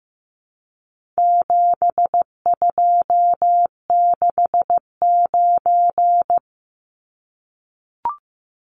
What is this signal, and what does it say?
Morse code, heard as a keyed tone, about 15 wpm, 700 Hz. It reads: 7269
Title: Numbers - 4-Digit 15wpm